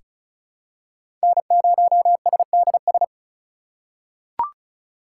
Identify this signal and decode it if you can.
Morse code, heard as a keyed tone, about 34 wpm, 690 Hz. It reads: N0SDS
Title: USA Call Signs - 5 Characters - 2x Character Spacing - 35wpm